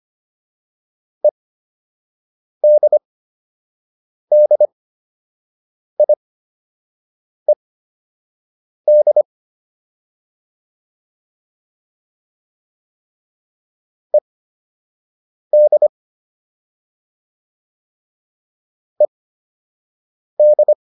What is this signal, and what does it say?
Morse code, heard as a keyed tone, about 25 wpm, spaced as for 6 wpm, 600 Hz. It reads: EDDIED ED ED